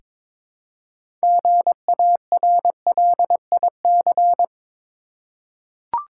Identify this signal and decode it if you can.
Morse code, heard as a keyed tone, about 22 wpm, 700 Hz. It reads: GARLIC
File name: Difficult Words with Hints - 22wpm